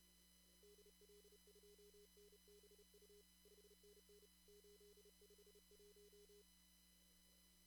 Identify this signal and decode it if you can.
Morse code, heard as a keyed tone, about 32 wpm, 410 Hz. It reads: DL2NBU 5NN 851
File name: CW / kt5j wpx cw-op-k5pi